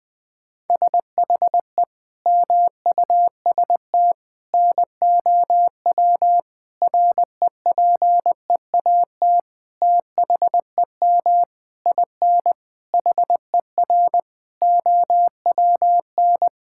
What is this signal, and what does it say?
Morse code, heard as a keyed tone, about 20 wpm, 700 Hz. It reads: SHE MUST NOW REPEAT THEM IN HER OWN
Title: Sentences from Top 1600 Words 20wpm